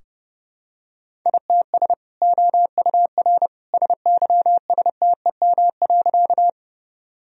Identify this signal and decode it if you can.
Morse code, heard as a keyed tone, about 30 wpm, 700 Hz. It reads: ITS OUR SYSTEM.
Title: Sentences from Top 1000 Words - Review Part 3 - 30wpm